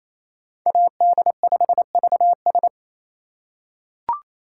Morse code, heard as a keyed tone, about 28 wpm, 700 Hz. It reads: AD5VS
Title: USA Call Signs - 5 Characters - 2x Character Spacing - 28wpm